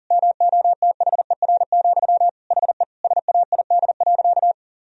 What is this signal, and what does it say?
Morse code, heard as a keyed tone, about 40 wpm, 700 Hz. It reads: MOTHER, HE SAID.
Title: Sentences from Top 1000 Words - Review Part 4 - 40wpm